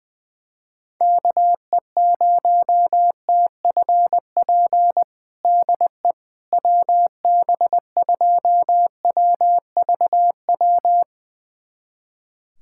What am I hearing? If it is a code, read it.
Morse code, heard as a keyed tone, about 21 wpm, 700 Hz. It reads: KE0TFP DE WB2WVW
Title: USA Callsign DE Callsign - 6 Characters - 20wpm